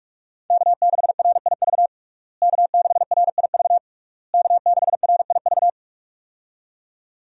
Morse code, heard as a keyed tone, about 45 wpm, 700 Hz. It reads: K6RIV K6RIV K6RIV
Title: USA Call Signs - 5 Characters - Repeated 3x - 45wpm